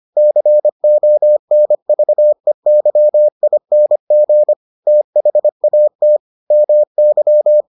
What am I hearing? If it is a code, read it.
Morse code, heard as a keyed tone, about 25 wpm, 600 Hz. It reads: CONVEYING THAT MY